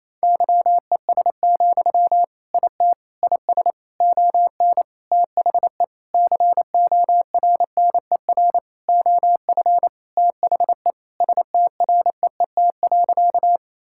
Morse code, heard as a keyed tone, about 28 wpm, 700 Hz. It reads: YES, IT IS ON THE CORNER OF THE STREET.
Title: Sentences from Top 1600 Words 28wpm